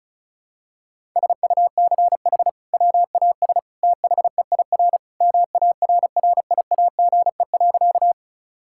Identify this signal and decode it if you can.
Morse code, heard as a keyed tone, about 36 wpm, 700 Hz. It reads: SUCH WAS THEIR MARRIAGE.
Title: Sentences from Top 1000 Words - Review Part 5 - 35wpm